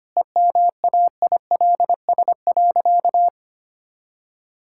Morse code, heard as a keyed tone, about 25 wpm, 690 Hz. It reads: EMAILS.
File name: Easy Sentences with Hints - 25wpm